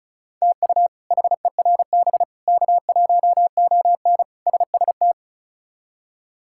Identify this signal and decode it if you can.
Morse code, heard as a keyed tone, about 34 wpm, 700 Hz. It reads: TU HERB K1ON SST
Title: K1USN SST QSO Practice - 35wpm